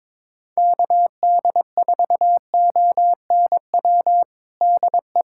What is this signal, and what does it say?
Morse code, heard as a keyed tone, about 22 wpm, 700 Hz. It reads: KD4ONW DE
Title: USA Callsign DE Callsign - 6 Characters - 22wpm